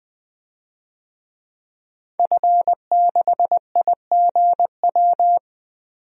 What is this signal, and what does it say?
Morse code, heard as a keyed tone, about 20 wpm, 700 Hz. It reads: F6IGW
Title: DXCC-100 Call Signs - 5 Characters - 20wpm